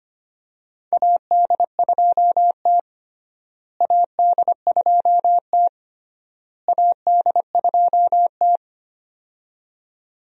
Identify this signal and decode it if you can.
Morse code, heard as a keyed tone, about 25 wpm, 700 Hz. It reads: AD2T AD2T AD2T
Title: USA Call Signs - 4 Characters - Repeated 3x - 25wpm